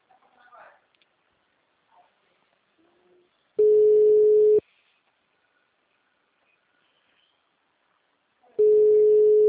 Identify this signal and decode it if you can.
Ringback tone (Europe, 425 Hz), with speech.
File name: out- -2051- - - .57581